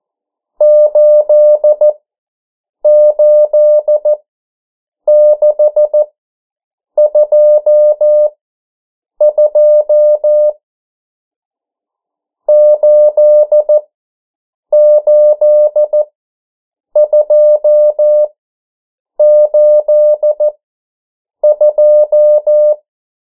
Morse code, heard as a keyed tone, about 14 wpm, spaced as for 7 wpm, 600 Hz. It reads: 88622 88282
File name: morse mp3 / 09 urok 8